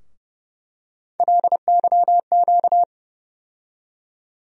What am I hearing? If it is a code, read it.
Morse code, heard as a keyed tone, about 31 wpm, 700 Hz. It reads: LYQ
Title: USA Call Sign Postfixes - 3 Letters 30wpm